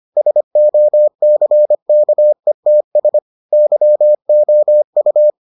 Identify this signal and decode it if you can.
Morse code, heard as a keyed tone, about 25 wpm, 600 Hz. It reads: SOCKETS YOU